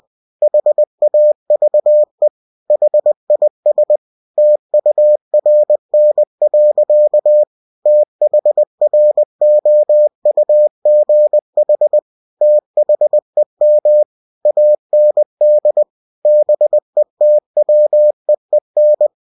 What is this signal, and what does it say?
Morse code, heard as a keyed tone, about 20 wpm, 600 Hz. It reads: HAVE HIS TURN. THROUGH THEM AND BETWEEN
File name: cw-books / winnie-the-pooh / winnie-20wpm-page-0028